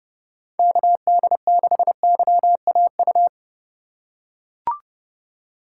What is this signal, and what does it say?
Morse code, heard as a keyed tone, about 32 wpm, 700 Hz. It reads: KD6YAU
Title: USA Call Signs - 6 Characters - 3x Character Spacing - 30wpm